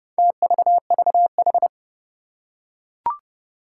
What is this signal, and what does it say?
Morse code, heard as a keyed tone, about 30 wpm, 700 Hz. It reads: TVVH
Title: Four Challenging Letter-Number Combinations 30wpm